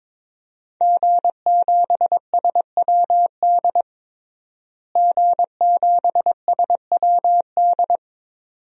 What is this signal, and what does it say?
Morse code, heard as a keyed tone, about 22 wpm, 700 Hz. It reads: G7SWD G7SWD
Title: DXCC-100 Call Signs - 5 Characters - Repeated 2x - 22wpm